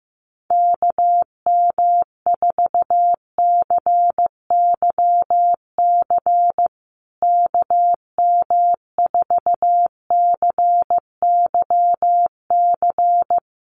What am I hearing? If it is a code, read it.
Morse code, heard as a keyed tone, about 15 wpm, 700 Hz. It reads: KM4CYC KM4CYC
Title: USA Call Signs Repeated 3x 15wpm